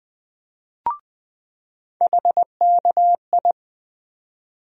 Morse code, heard as a keyed tone, about 21 wpm, 700 Hz. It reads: HKI
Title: USA Call Sign Postfixes - 3 Letters 20wpm